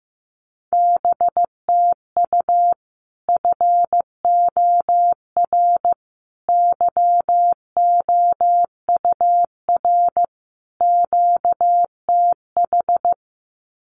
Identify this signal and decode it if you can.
Morse code, heard as a keyed tone, about 15 wpm, 700 Hz. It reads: BTU FOR YOUR QTH
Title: Common QSO Phrases 15wpm